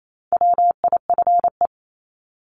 Morse code, heard as a keyed tone, about 28 wpm, 700 Hz. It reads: WIFE
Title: Top 1751-2000 Words 28wpm